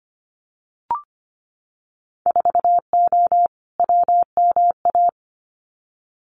Morse code, heard as a keyed tone, about 25 wpm, 700 Hz. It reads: 4O WMA
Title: Winter Field Day Exchanges Encore 25wpm